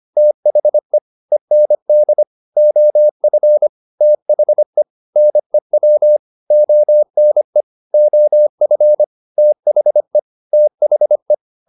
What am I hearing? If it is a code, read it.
Morse code, heard as a keyed tone, about 25 wpm, 600 Hz. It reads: THE END OF THE NEW ONE OF THE THE